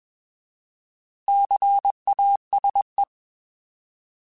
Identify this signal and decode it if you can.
Morse code, heard as a keyed tone, about 21 wpm, 800 Hz. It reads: CASE